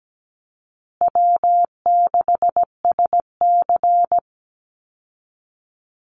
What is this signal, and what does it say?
Morse code, heard as a keyed tone, about 17 wpm, 700 Hz. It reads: W6SC
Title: Easy Call Signs Repeated 3x - 17wpm